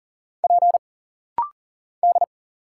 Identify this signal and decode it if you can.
Morse code, heard as a keyed tone, about 40 wpm, 700 Hz. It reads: P D
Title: Single Letters - Rapid-Fire 40wpm